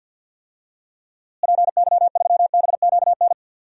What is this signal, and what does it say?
Morse code, heard as a keyed tone, about 53 wpm, 700 Hz. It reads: PY3BQN
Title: DXCC-100 Call Signs - 6 Characters - Repeated 2x - 50wpm